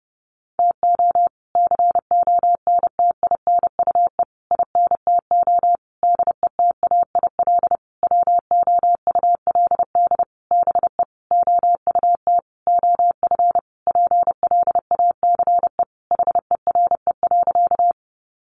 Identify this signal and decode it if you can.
Morse code, heard as a keyed tone, about 30 wpm, 700 Hz. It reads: TO CONTINUE INTO DETAIL WOULD BE OUT OF PLACE HERE.
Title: Sentences from Top 1500 Words 30wpm